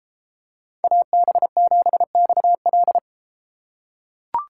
Morse code, heard as a keyed tone, about 32 wpm, 700 Hz. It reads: AB7XL